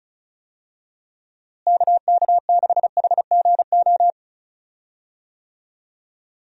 Morse code, heard as a keyed tone, about 34 wpm, 700 Hz. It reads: KK6HGO
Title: USA Call Signs - 6 Characters - Repeated 2x - 35wpm